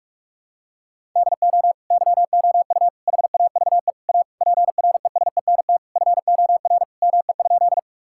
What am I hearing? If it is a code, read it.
Morse code, heard as a keyed tone, about 41 wpm, 700 Hz. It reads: DO YOU HAVE A PRESENT FOR ME?
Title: Sentences from Top 1000 Words - Review Part 5 - 45wpm